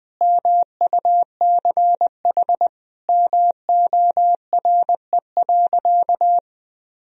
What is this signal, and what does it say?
Morse code, heard as a keyed tone, about 21 wpm, 700 Hz. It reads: MUCH MORE.
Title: Sentences from Top 1000 Words - Review Part 4 - 20wpm